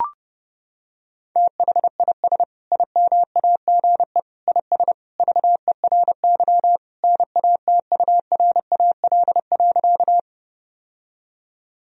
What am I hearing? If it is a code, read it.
Morse code, heard as a keyed tone, about 30 wpm, 700 Hz. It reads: THIS IMAGE IS VERY NATURAL.
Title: Sentences from Top 1000 Words - Review Part 4 - 30wpm